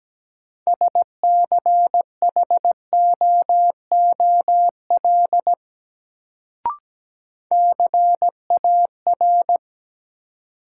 Morse code, heard as a keyed tone, about 17 wpm, 700 Hz. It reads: SCHOOL CAR
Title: Sentences from Top 300 Words - New Word Drill - 17wpm